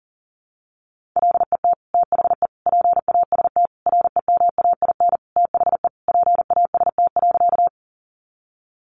Morse code, heard as a keyed tone, about 40 wpm, 700 Hz. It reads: LET THE PAST REMAIN THE PAST.
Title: Sentences from Top 1200 Words 40wpm